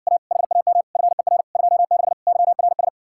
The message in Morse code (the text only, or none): A HAM FER 26 YRS